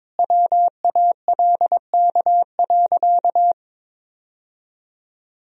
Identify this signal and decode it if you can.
Morse code, heard as a keyed tone, about 22 wpm, 700 Hz. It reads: WALK.
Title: Sentences from Top 1000 Words - Review Part 1 - 22wpm